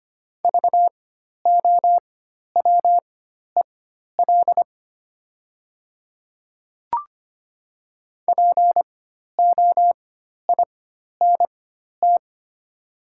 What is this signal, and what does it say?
Morse code, heard as a keyed tone, about 25 wpm, spaced as for 12 wpm, 700 Hz. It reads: VOWEL POINT
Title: 4x Character Spacing - Top 500 Words - 25wpm